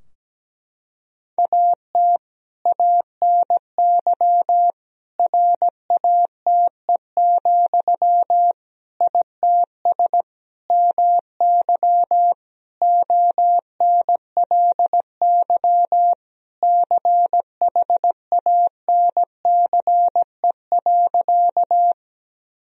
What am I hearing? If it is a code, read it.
Morse code, heard as a keyed tone, about 17 wpm, 700 Hz. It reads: AT ANY RATE, ITS MY ONLY CHANCE.
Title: Sentences from Top 1000 Words - Review Part 4 - 17wpm